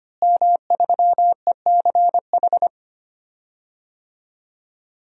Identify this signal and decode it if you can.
Morse code, heard as a keyed tone, about 25 wpm, 700 Hz. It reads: M3ECH
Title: DXCC-100 Call Signs - 5 Characters - Repeated 2x - 25wpm